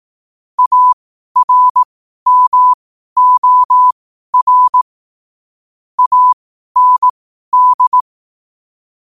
Morse code, heard as a keyed tone, about 18 wpm, spaced as for 12 wpm, 1000 Hz. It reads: ARMOR AND
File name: 12wpm